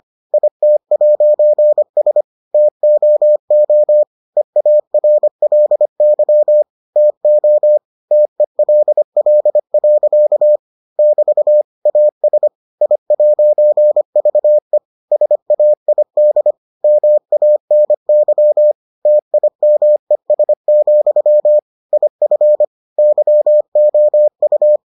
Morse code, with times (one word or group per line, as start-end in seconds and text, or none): 0.3-2.2 IT'S
2.5-4.0 TOO
4.4-6.6 EARLY
7.0-7.8 TO
8.1-10.6 TELL.
11.0-11.6 <BT>
11.9-12.5 AS
12.8-14.8 I'VE
15.1-16.5 SAID
16.8-18.7 MANY
19.1-21.6 TIMES,
21.9-22.6 IF
23.0-24.8 YOU